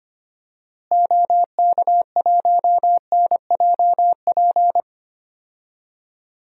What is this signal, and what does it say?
Morse code, heard as a keyed tone, about 26 wpm, 700 Hz. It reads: OK1NJP
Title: DXCC-100 Call Signs - 6 Characters - Repeated 3x - 25wpm